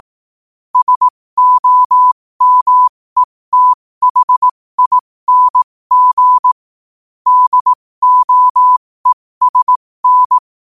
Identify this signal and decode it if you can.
Morse code, heard as a keyed tone, about 18 wpm, spaced as for 15 wpm, 1000 Hz. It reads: SOMETHING DOESN
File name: qotdfiles / 15wpm